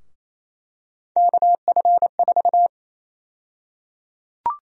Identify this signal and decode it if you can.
Morse code, heard as a keyed tone, about 28 wpm, 700 Hz. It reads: KF4
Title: USA Call Sign Prefixes - Repeated 2x - 28wpm